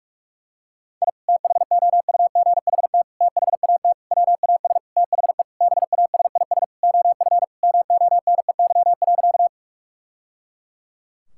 Morse code, heard as a keyed tone, about 46 wpm, 700 Hz. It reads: I THOUGHT THAT WAS THE BASIS OF MONEY.
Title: Sentences from Top 1000 Words - Review Part 5 - 45wpm